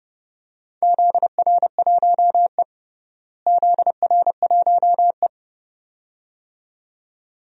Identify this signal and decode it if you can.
Morse code, heard as a keyed tone, about 30 wpm, 700 Hz. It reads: ZR1E ZR1E
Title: DXCC-100 Call Signs - 4 Characters - Repeated 2x - 30wpm